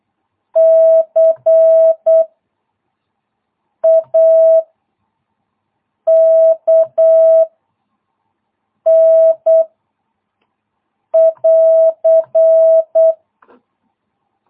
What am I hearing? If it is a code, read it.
Morse code, heard as a keyed tone, about 8 wpm, 660 Hz. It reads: C A K N <AR>